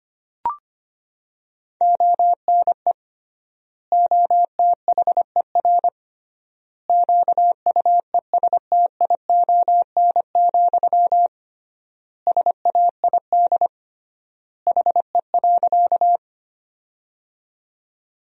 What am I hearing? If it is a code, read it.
Morse code, heard as a keyed tone, about 25 wpm, 700 Hz. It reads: ONE OTHER QUESTION, SAID HE.
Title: 3x Word-Spacing - Sentences from Top 400 Words - 25wpm